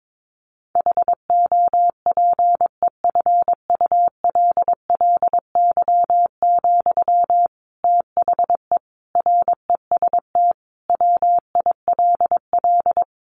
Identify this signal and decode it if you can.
Morse code, heard as a keyed tone, about 22 wpm, 700 Hz. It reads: HOPEFULLY, THE REST WILL
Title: Sentences from Top 1500 Words 22wpm